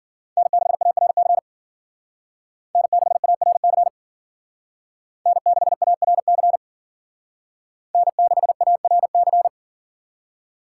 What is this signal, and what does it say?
Morse code, heard as a keyed tone, about 52 wpm, 700 Hz. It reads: N6ARC N6ARC N6ARC N6ARC